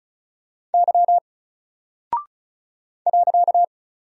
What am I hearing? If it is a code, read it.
Morse code, heard as a keyed tone, about 34 wpm, 700 Hz. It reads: Y .
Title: Single Character (Letter:Number:Punctuation:Prosign) Alt 35wpm